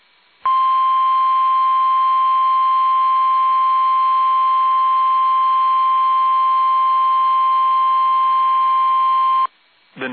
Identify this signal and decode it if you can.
Emergency Alert System (SAME) tones (the 1050 Hz attention tone). It is heard from 0.4 to 9.5 s.
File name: eas / KUSO / KUSO